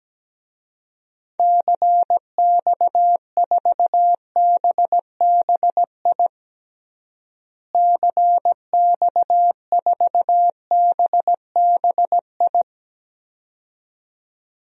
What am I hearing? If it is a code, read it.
Morse code, heard as a keyed tone, about 17 wpm, 700 Hz. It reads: CX4BBI CX4BBI
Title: DXCC-100 Call Signs - 6 Characters - Repeated 2x - 17wpm